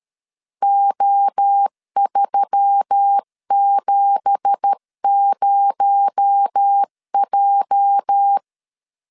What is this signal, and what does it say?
Morse code, heard as a keyed tone, about 13 wpm, 800 Hz. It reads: O370J
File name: LEZIONI / 10 SIGNS / 10B INT 12wpm 2414